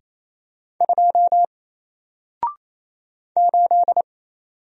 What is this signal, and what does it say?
Morse code, heard as a keyed tone, about 28 wpm, 700 Hz. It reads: 2 8
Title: Single Character (Letter:Number:Punctuation:Prosign) Alt 28wpm